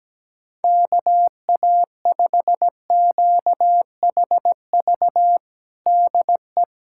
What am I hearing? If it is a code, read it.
Morse code, heard as a keyed tone, about 17 wpm, 700 Hz. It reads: KA5QHV DE
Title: USA Callsign DE Callsign - 6 Characters - 17wpm